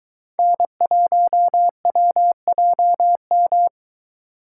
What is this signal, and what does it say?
Morse code, heard as a keyed tone, about 23 wpm, 700 Hz. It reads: N1WJM